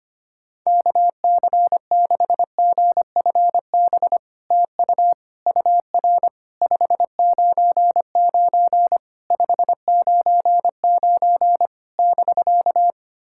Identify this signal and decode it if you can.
Morse code, heard as a keyed tone, about 24 wpm, 700 Hz. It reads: KC6GFB TU UR 599 599 <BK>
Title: SOTA QSO Practice 25wpm